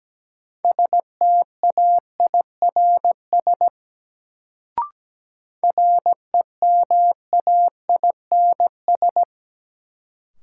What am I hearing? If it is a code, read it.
Morse code, heard as a keyed tone, about 17 wpm, 700 Hz. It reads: STAIRS REMAINS
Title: Sets of 5 Words Encore - New Word Drill - 17wpm